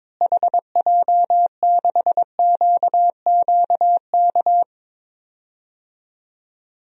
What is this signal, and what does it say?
Morse code, heard as a keyed tone, about 21 wpm, 700 Hz. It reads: HJ6QQK
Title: DXCC-100 Call Signs - 6 Characters - Repeated 3x - 22wpm